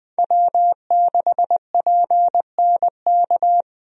Morse code, heard as a keyed tone, about 20 wpm, 700 Hz. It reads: W6PNK
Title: USA Call Signs - 5 Characters - 20wpm